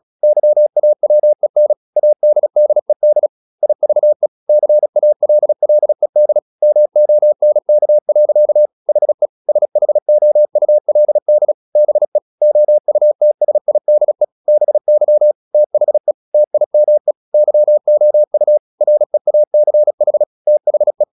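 Morse code, heard as a keyed tone, about 36 wpm, 600 Hz. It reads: YAWEN ADDED IVE CALLED MONK. HE SHOULD BE OUTSIDE BY THE TIME YOU REACH THE
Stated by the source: Morse practice at 35 wpm from CW med Gnister, sending Ken Follett's novel Never in English.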